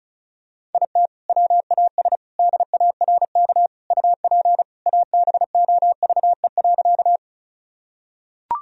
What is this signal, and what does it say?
Morse code, heard as a keyed tone, about 36 wpm, 700 Hz. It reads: IT WAS DARK UP ABOVE.
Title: Sentences from Top 1000 Words - Review Part 2 - 35wpm